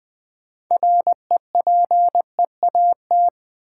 Morse code, heard as a keyed tone, about 20 wpm, 700 Hz. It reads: REPEAT
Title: Sentences from Top 1600 Words - New Word Drill - 20wpm